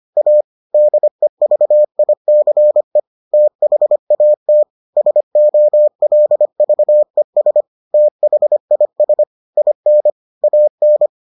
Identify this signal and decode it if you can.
Morse code, heard as a keyed tone, about 25 wpm, 600 Hz. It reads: A DEVICE THAT SOLVES THIS IN AN